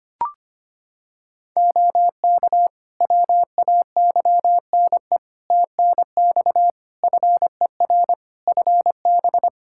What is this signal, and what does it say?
Morse code, heard as a keyed tone, about 25 wpm, 700 Hz. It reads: OK WAYNE TNX FER FB
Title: Practice QSO - Exchange 4 - Ending - 25wpm